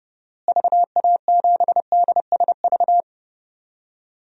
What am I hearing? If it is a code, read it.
Morse code, heard as a keyed tone, about 30 wpm, 700 Hz. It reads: VA7DSV
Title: DXCC-100 Call Signs - 6 Characters - Repeated 3x - 30wpm